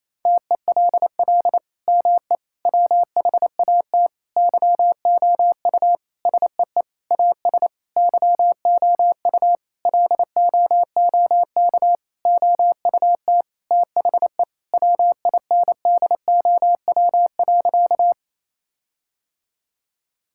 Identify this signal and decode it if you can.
Morse code, heard as a keyed tone, about 28 wpm, 700 Hz. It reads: TELL ME WHAT YOU SEE AS YOU LOOK OUT THE WINDOW.
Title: Sentences from Top 1600 Words 28wpm